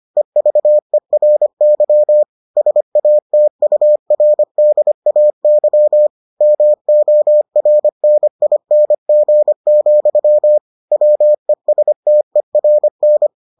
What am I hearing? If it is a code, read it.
Morse code, heard as a keyed tone, about 25 wpm, 600 Hz. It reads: EVERY SATURDAY MORNING, WESTERN